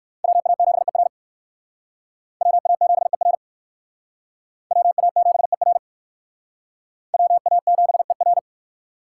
Morse code, heard as a keyed tone, about 58 wpm, 700 Hz. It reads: WA7ER WA7ER WA7ER WA7ER